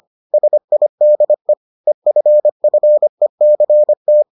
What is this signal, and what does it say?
Morse code, heard as a keyed tone, about 25 wpm, 600 Hz. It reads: SIDE EFFECT